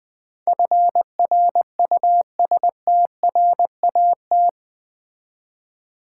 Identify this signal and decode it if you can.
Morse code, heard as a keyed tone, about 20 wpm, 700 Hz. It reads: FRUSTRAT
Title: Build A Word - 4 Letters At A Time - 10-Letter Words - 20wpm